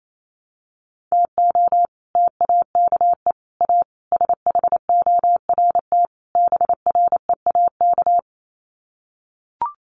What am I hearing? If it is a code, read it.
Morse code, heard as a keyed tone, about 28 wpm, 700 Hz. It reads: TO TAKE A SHORT BREAK
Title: Sets of 5 Words Encore2 28wpm